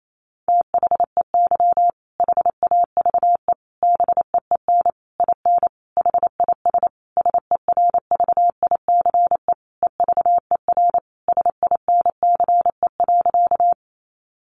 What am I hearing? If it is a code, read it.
Morse code, heard as a keyed tone, about 28 wpm, 700 Hz. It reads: THEY HAVE BEEN IN HIS SERVICE EVER SINCE.
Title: Sentences from Top 700 Words 28wpm